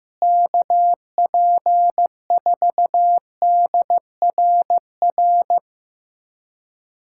Morse code, heard as a keyed tone, about 15 wpm, 700 Hz. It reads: KP4DRR
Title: USA Callsign DE Callsign - 6 Characters - 15wpm